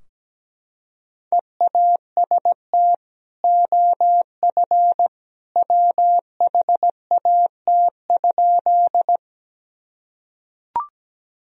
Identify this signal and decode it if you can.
Morse code, heard as a keyed tone, about 17 wpm, 700 Hz. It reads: EAST OF WHAT?
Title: Sentences from Top 1000 Words - Review Part 3 - 17wpm